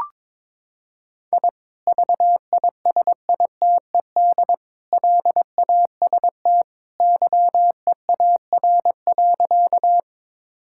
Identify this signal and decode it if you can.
Morse code, heard as a keyed tone, about 22 wpm, 700 Hz. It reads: I VISITED LAST YEAR.
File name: practice / 22wpm / Sentences with 2-Word Phrases - 22wpm